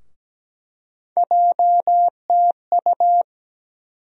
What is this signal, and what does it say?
Morse code, heard as a keyed tone, about 17 wpm, 700 Hz. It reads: JTU
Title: USA Call Sign Postfixes - 3 Letters 17wpm